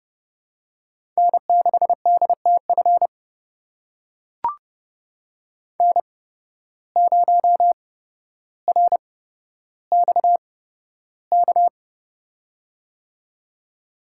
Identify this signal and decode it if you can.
Morse code, heard as a keyed tone, about 30 wpm, spaced as for 13 wpm, 700 Hz. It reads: N6DTF N0RXK
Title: USA Call Signs - 8x Character Spacing - 30wpm